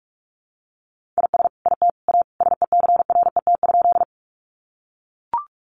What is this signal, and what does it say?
Morse code, heard as a keyed tone, about 44 wpm, 700 Hz. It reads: IS IT A SECRET?
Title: Sentences from Top 1200 Words 45wpm